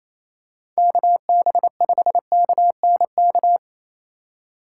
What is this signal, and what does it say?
Morse code, heard as a keyed tone, about 27 wpm, 700 Hz. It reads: KB5KNK
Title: USA Call Signs - 3x Character Spacing - 28wpm